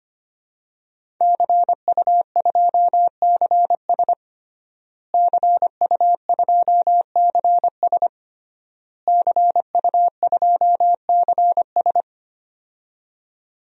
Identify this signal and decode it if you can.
Morse code, heard as a keyed tone, about 25 wpm, 700 Hz. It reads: CU2CS CU2CS CU2CS
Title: DXCC-100 Call Signs - 5 Characters - Repeated 3x - 25wpm